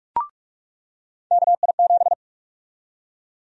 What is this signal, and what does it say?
Morse code, heard as a keyed tone, about 46 wpm, 700 Hz. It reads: KI7
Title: USA Call Sign Prefixes - 45wpm